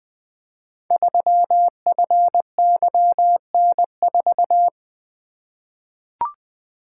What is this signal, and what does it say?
Morse code, heard as a keyed tone, about 20 wpm, 700 Hz. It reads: 3FYN4
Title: Code Groups - 5 Letter:Numbers - 20wpm